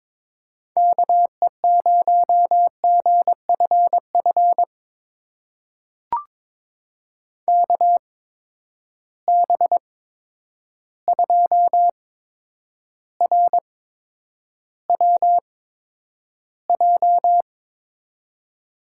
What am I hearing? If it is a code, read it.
Morse code, heard as a keyed tone, about 22 wpm, spaced as for 9 wpm, 700 Hz. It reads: KE0GFF KB2RWJ
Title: USA Call Signs - 6 Characters - 8x Character Spacing - 22wpm